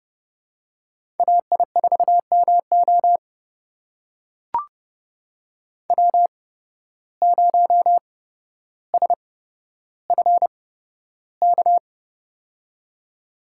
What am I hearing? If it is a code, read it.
Morse code, heard as a keyed tone, about 29 wpm, spaced as for 13 wpm, 700 Hz. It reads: AI4MO W0SFK
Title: USA Call Signs - 5 Characters - 8x Character Spacing - 30wpm